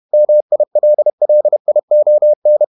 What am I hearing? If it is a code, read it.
Morse code, heard as a keyed tone, about 31 wpm, 600 Hz. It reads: MILLION
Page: Never 31wpm | CW med Gnister